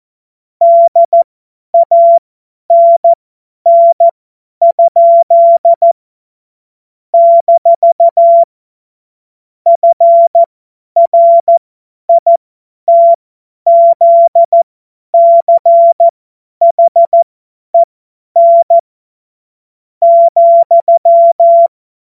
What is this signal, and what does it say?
Morse code, heard as a keyed tone, about 14 wpm, spaced as for 9 wpm, 680 Hz. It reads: DANN? - FRITZCHEN,
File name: Morsekurs / Witze / Witz 10. August 2021, 50 F70